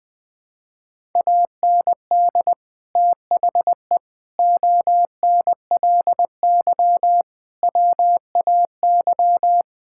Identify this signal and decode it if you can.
Morse code, heard as a keyed tone, about 20 wpm, 700 Hz. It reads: AND THE ONLY WAY
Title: Sets of 9 Words - 20wpm